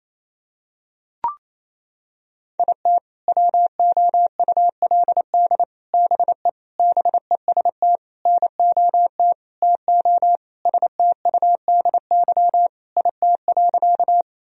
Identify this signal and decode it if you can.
Morse code, heard as a keyed tone, about 28 wpm, 700 Hz. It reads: IT WOULD BE BEST NOT TO STUDY IT.
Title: Sentences from Top 1000 Words - Review Part 1 - 28wpm